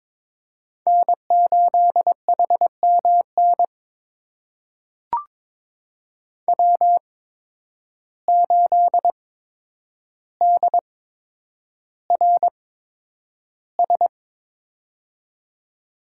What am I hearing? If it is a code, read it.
Morse code, heard as a keyed tone, about 22 wpm, spaced as for 9 wpm, 700 Hz. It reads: N8HMN W8DRS
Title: USA Call Signs - 5 Characters - 8x Character Spacing - 22wpm